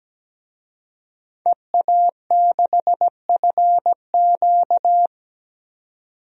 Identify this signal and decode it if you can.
Morse code, heard as a keyed tone, about 17 wpm, 700 Hz. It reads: EA6FQ
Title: DXCC-100 Call Signs - 5 Characters - 17wpm